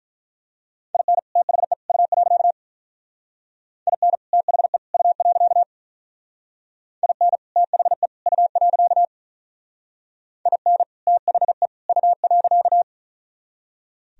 Morse code, heard as a keyed tone, about 43 wpm, 700 Hz. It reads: IN THE U. IN THE U. IN THE U. IN THE U.
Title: Speed-Racing - Sets of 3 Words 35wpm